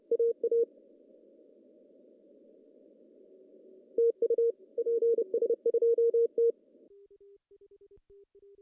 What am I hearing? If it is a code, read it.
Morse code, heard as a keyed tone, about 30 wpm, 460 Hz. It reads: AA TU PS2T
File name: CW / cqww-cw-k5tr